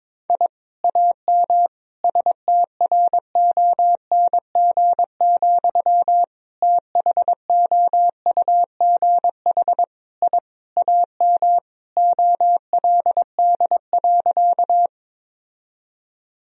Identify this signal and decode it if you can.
Morse code, heard as a keyed tone, about 22 wpm, 700 Hz. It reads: I AM STRONG, THOUGH I AM OLD.
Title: Sentences from Top 1000 Words - Review Part 3 - 22wpm